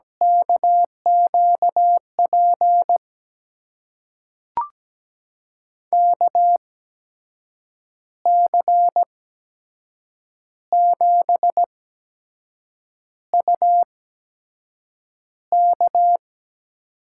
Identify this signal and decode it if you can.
Morse code, heard as a keyed tone, about 17 wpm, spaced as for 6 wpm, 700 Hz. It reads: KQP KC7UK
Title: USA Call Signs - 6 Characters - 8x Character Spacing - 17wpm